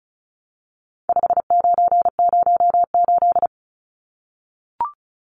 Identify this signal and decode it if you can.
Morse code, heard as a keyed tone, about 34 wpm, 700 Hz. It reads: 5908
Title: Numbers - 4-Digit 35wpm